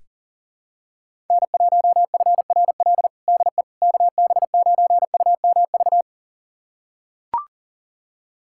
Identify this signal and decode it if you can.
Morse code, heard as a keyed tone, about 37 wpm, 700 Hz. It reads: N1FRL DE KB9UMV
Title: USA Callsign DE Callsign 40wpm